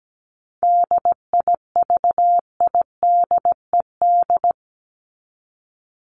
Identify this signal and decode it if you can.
Morse code, heard as a keyed tone, about 17 wpm, 700 Hz. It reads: DIVIDED
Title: ED Suffix - Top 100 Words with Base Words 17wpm